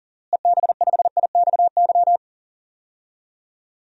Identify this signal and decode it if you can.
Morse code, heard as a keyed tone, about 37 wpm, 700 Hz. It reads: EB5IXY
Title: DXCC-100 Call Signs - 6 Characters - Repeated 3x - 40wpm